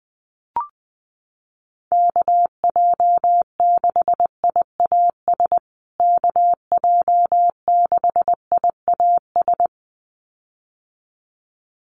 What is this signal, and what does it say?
Morse code, heard as a keyed tone, about 20 wpm, 700 Hz. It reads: KJ6IAS KJ6IAS
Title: USA Call Signs Repeated 2x 20wpm